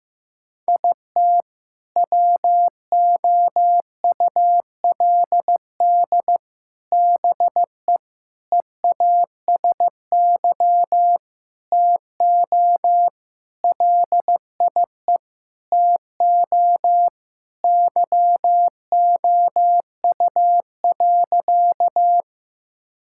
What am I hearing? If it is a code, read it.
Morse code, heard as a keyed tone, about 15 wpm, 700 Hz. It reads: IT WOULD BE EASY TO LIE TO YOU.
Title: Sentences from Top 1600 Words 15wpm